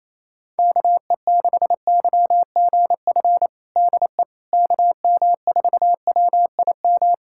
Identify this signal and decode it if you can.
Morse code, heard as a keyed tone, about 28 wpm, 700 Hz. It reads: KE6YGF DE KM4WIM
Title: USA Callsign DE Callsign - 6 Characters - 28wpm